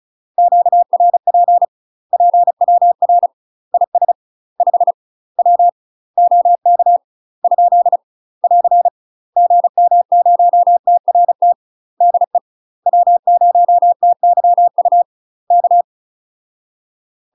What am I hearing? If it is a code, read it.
Morse code, heard as a keyed tone, about 34 wpm, 700 Hz. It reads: QRP PWR IS 5 W OK ? <AR> GM0TRT DE W0TYU K
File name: UK MASTER QSOs TRAINING / 35WPM / UK MASTER QSOs 0030